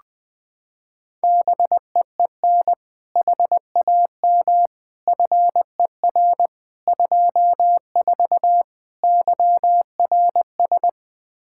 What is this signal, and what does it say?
Morse code, heard as a keyed tone, about 20 wpm, 700 Hz. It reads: BEEN HAM FER 24 YRS
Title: Practice QSO - Exchange 3 - More Info - 20wpm